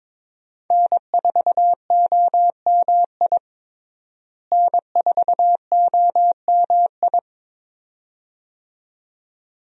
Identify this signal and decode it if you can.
Morse code, heard as a keyed tone, about 22 wpm, 700 Hz. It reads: N4OMI N4OMI
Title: USA Call Signs - 5 Characters - Repeated 2x - 22wpm